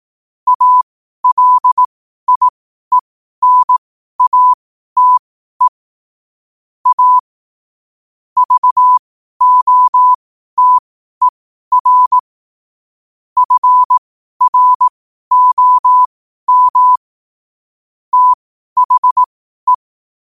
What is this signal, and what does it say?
Morse code, heard as a keyed tone, about 18 wpm, spaced as for 12 wpm, 1000 Hz. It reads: ALIENATE A VOTER FROM THE
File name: qotdfiles / 12wpm